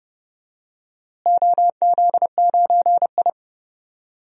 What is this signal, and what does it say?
Morse code, heard as a keyed tone, about 30 wpm, 700 Hz. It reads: OZ9I
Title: DXCC-100 Call Signs - 4 Characters - Repeated 3x - 30wpm